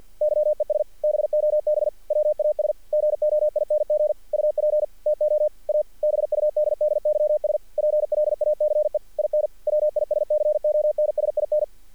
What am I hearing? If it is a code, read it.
Morse code, heard as a keyed tone, about 48 wpm, 600 Hz. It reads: YES BOB WAS GOING UP TO A BUDDYS PLACE IN WISCONSIN
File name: archived / CRSnet